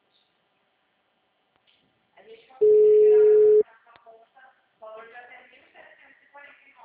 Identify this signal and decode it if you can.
Ringback tone (Europe, 425 Hz), with speech.